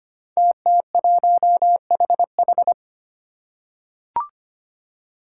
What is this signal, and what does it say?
Morse code, heard as a keyed tone, about 25 wpm, 700 Hz. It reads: TT1HH
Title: Five Challenging Letter-Number Combinations 25wpm